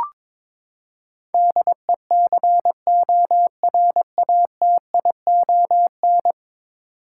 Morse code, heard as a keyed tone, about 22 wpm, 700 Hz. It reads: DECORATION
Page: morse-code-ninja-practice-22wpm